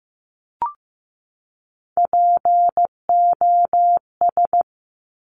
Morse code, heard as a keyed tone, about 15 wpm, 700 Hz. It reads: POS
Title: Top Three Letter Combinations in Words 15wpm